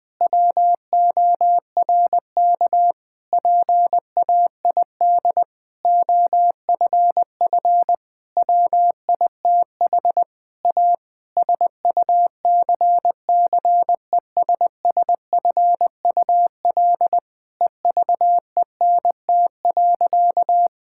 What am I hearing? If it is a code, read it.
Morse code, heard as a keyed tone, about 21 wpm, 700 Hz. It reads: WORK PAID OFF WITH A SUCCESSFUL EVENT.
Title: Difficult Sentences with Hints - 20wpm